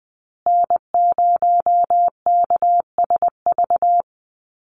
Morse code, heard as a keyed tone, about 20 wpm, 700 Hz. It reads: N0KSV
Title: USA Call Signs Repeated 2x 20wpm